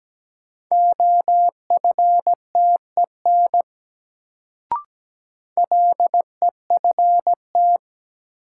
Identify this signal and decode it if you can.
Morse code, heard as a keyed tone, about 17 wpm, 700 Hz. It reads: OFTEN LEFT
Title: Sentences from Top 300 Words - New Word Drill - 17wpm